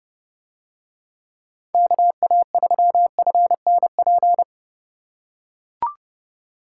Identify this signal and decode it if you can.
Morse code, heard as a keyed tone, about 29 wpm, 700 Hz. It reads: KA3FNP